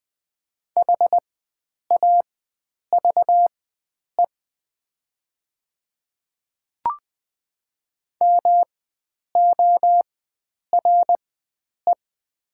Morse code, heard as a keyed tone, about 20 wpm, spaced as for 9 wpm, 690 Hz. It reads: HAVE MORE
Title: 4x Character Spacing - Top 100 Words - 20wpm